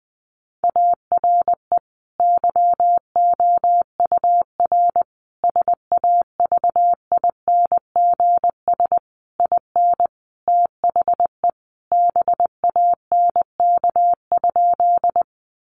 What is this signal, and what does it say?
Morse code, heard as a keyed tone, about 20 wpm, 700 Hz. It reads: ARE YOUR SAVINGS IN THE BANK?
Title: Sentences from Top 1200 Words 20wpm